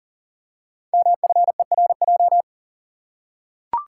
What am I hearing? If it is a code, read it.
Morse code, heard as a keyed tone, about 39 wpm, 690 Hz. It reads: MFERJ
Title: Code Groups - 5 Letters - 40wpm